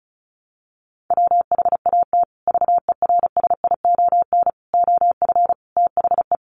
Morse code, heard as a keyed tone, about 34 wpm, 700 Hz. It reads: WHAT VERSION OF THE
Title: Sentences from Top 1400 Words 35wpm